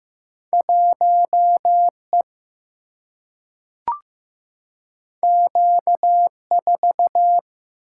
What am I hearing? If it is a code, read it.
Morse code, heard as a keyed tone, about 15 wpm, 700 Hz. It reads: 1E Q4
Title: Single-Letter Single-Number 15wpm